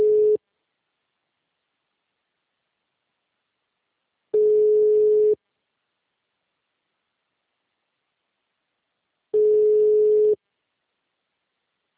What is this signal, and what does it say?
Ringback tone (Europe, 425 Hz).